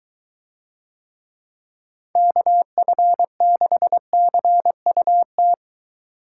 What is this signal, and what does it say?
Morse code, heard as a keyed tone, about 20 wpm, 700 Hz. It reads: KF6CUT